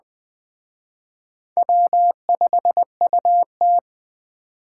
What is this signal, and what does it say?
Morse code, heard as a keyed tone, about 21 wpm, 700 Hz. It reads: W5UT
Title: Speed Racing - USA Call Signs 17wpm